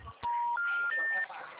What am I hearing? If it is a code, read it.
Special information tone, with speech.